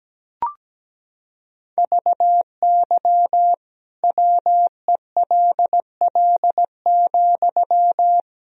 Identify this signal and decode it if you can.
Morse code, heard as a keyed tone, about 17 wpm, 700 Hz. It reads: VY WELL,
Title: Sentences from Top 500 Words and Abbreviations 17wpm